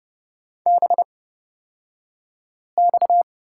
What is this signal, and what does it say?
Morse code, heard as a keyed tone, about 30 wpm, spaced as for 5 wpm, 700 Hz. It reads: BX
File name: Lesson 37d - Letter x - Two Characters 30wpm